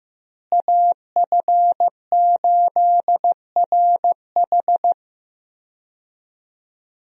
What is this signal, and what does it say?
Morse code, heard as a keyed tone, about 16 wpm, 700 Hz. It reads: AF8RH
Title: USA Call Signs - 5 Characters - Repeated 2x - 15wpm